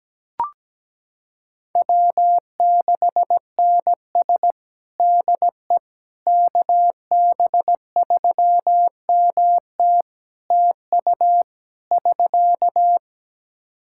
Morse code, heard as a keyed tone, about 17 wpm, 700 Hz. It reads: W6NS DE KB3MT TU <SK>
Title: Practice QSO - Exchange 4 - Ending - 17wpm